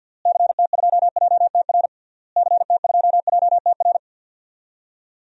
Morse code, heard as a keyed tone, about 52 wpm, 700 Hz. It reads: CT2JTR CT2JTR
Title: DXCC-100 Call Signs - 6 Characters - Repeated 3x - 50wpm